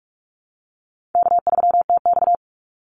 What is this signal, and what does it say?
Morse code, heard as a keyed tone, about 44 wpm, 700 Hz. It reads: K3TX
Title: Call Signs - 45wpm